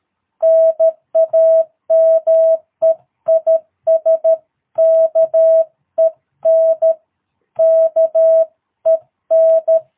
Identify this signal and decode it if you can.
Morse code, heard as a keyed tone, about 13 wpm, 650 Hz. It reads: NAME IS KEN KEN
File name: jh8cbh / 2022cw